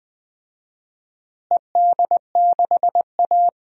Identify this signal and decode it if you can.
Morse code, heard as a keyed tone, about 20 wpm, 700 Hz. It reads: ED6A
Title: DXCC-100 Call Signs - 4 Characters - Repeated 2x - 20wpm